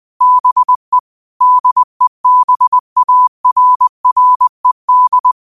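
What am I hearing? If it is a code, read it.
Morse code, heard as a keyed tone, about 20 wpm, 1000 Hz. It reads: BE DEBARRED
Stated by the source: Morse code quote of the day at 20 wpm.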